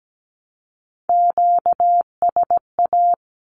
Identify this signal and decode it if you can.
Morse code, heard as a keyed tone, about 17 wpm, 700 Hz. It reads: QSA
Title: Q-Codes 17wpm